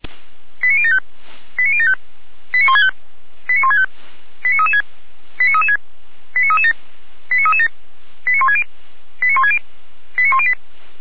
Five-tone selective call (ZVEI), heard as quick runs of five tones, tones of about 70 ms. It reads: 89075 89075 89165 89165 89336 89336 89337 89337 89160 89160 89108
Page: SDR/ZVEI